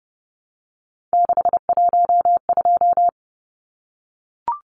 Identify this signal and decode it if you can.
Morse code, heard as a keyed tone, about 30 wpm, 700 Hz. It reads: 612
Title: Numbers - 3-Digit 30wpm